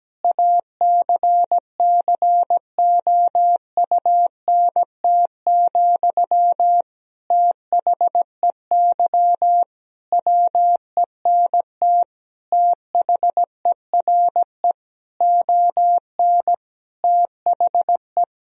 Morse code, heard as a keyed tone, about 17 wpm, 700 Hz. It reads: ACCOUNT, THEY WENT THERE ON THE
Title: Sentences from Top 1000 Words - Review Part 4 - 17wpm